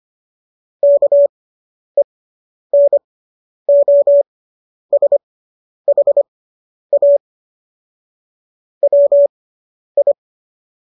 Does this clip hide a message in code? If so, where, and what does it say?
Yes: Morse code from 0.8 to 10.1 s, heard as a keyed tone, about 25 wpm, spaced as for 10 wpm, 570 Hz. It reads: KENOSHA WI